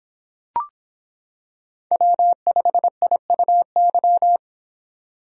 Morse code, heard as a keyed tone, about 25 wpm, 700 Hz. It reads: W5IUY